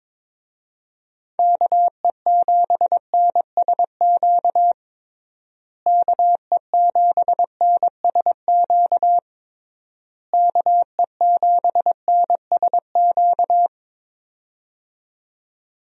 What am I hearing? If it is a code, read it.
Morse code, heard as a keyed tone, about 22 wpm, 700 Hz. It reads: KE7NSQ KE7NSQ KE7NSQ
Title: USA Call Signs - 6 Characters - Repeated 3x - 22wpm